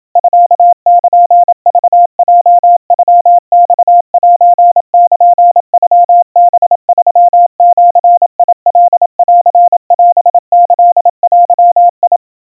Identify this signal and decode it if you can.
Morse code (Japanese Wabun code), heard as a keyed tone, about 27 wpm, 700 Hz. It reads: ミルクヲノマセルノハ3ジカンオキデ
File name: kawa / Morse sample / kamyu 100 cpm 700 Hz Crispy